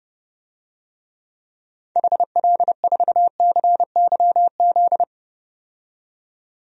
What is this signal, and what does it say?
Morse code, heard as a keyed tone, about 30 wpm, 700 Hz. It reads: HL4CYZ
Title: DXCC-100 Call Signs - 6 Characters - 30wpm